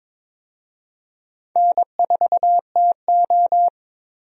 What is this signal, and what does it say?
Morse code, heard as a keyed tone, about 22 wpm, 700 Hz. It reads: N4TO
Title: USA Call Signs - 4 Characters - 22wpm